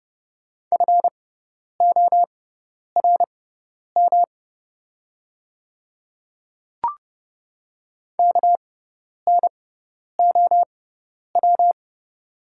Morse code, heard as a keyed tone, about 30 wpm, spaced as for 10 wpm, 690 Hz. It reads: FORM KNOW
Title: 6x Character Spacing - Top 200 Words - 30wpm